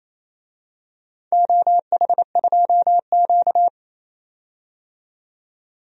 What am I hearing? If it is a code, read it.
Morse code, heard as a keyed tone, about 28 wpm, 700 Hz. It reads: OH2Q
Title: DXCC-100 Call Signs - 4 Characters - 28wpm